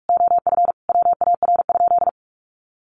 Morse code, heard as a keyed tone, about 44 wpm, 700 Hz. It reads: OF WAR?
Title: Sentences from Top 1300 Words 45wpm